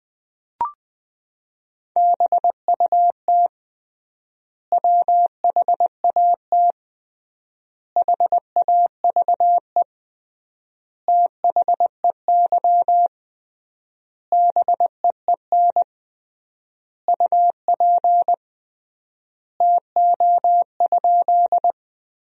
Morse code, heard as a keyed tone, about 20 wpm, 700 Hz. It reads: BUT WHAT HAVE THEY BEEN UP TO?
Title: 3x Word-Spacing - Sentences from Top 100 Words - 20wpm